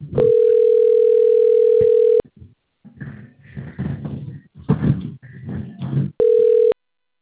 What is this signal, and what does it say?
Ringback tone (North America), 440 + 480 Hz.